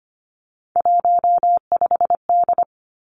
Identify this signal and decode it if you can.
Morse code, heard as a keyed tone, about 25 wpm, 700 Hz. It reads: 15D
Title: Three Challenging Letter-Number Combinations 25wpm